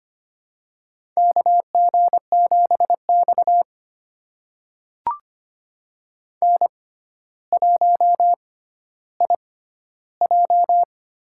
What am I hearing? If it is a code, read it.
Morse code, heard as a keyed tone, about 25 wpm, spaced as for 13 wpm, 700 Hz. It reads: KG7X N1IJ
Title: USA Call Signs - 4 Characters - 6x Character Spacing - 25wpm